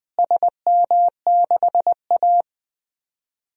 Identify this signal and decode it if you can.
Morse code, heard as a keyed tone, about 21 wpm, 700 Hz. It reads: SM6A
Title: DXCC-100 Call Signs - 4 Characters - Repeated 3x - 20wpm